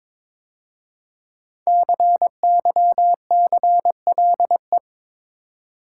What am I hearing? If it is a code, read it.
Morse code, heard as a keyed tone, about 22 wpm, 700 Hz. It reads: CYCLE
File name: 5 Letter Words - First 3 Letters Given - 22wpm